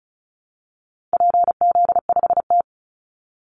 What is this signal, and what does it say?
Morse code, heard as a keyed tone, about 34 wpm, 700 Hz. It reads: PZ5T
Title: Call Signs Repeated 2x - 35wpm